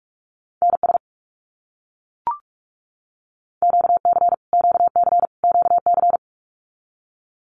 Morse code, heard as a keyed tone, about 44 wpm, 700 Hz. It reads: NS QC QC QC
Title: Canadian Provinces and Territories - Repeated 3x - 45wpm